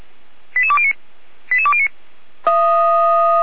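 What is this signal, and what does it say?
Five-tone selective call (ZVEI), heard as quick runs of five tones, tones of about 80 ms. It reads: 88209 88209
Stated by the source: ZVEI five-tone selective call 88209 received via SDR.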